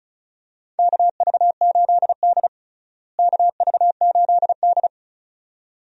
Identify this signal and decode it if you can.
Morse code, heard as a keyed tone, about 34 wpm, 700 Hz. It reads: KV8D KV8D
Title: USA Call Signs - 4 Characters - Repeated 2x - 35wpm